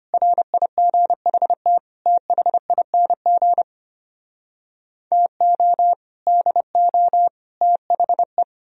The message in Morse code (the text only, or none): RIGHT THING TO DO THE